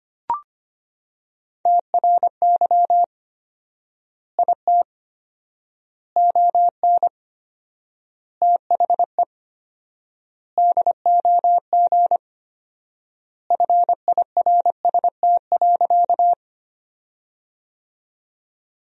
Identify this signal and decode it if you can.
Morse code, heard as a keyed tone, about 25 wpm, 700 Hz. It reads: TRY IT ON THE DOG FIRST.
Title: 4x Word-Spacing - Sentences from Top 400 Words - 25wpm